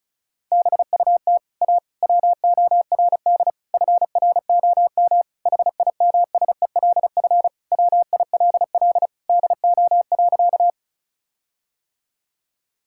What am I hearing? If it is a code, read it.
Morse code, heard as a keyed tone, about 34 wpm, 700 Hz. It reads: BUT A WORD FROM HIMSELF WILL DO.
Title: Sentences from Top 1600 Words 35wpm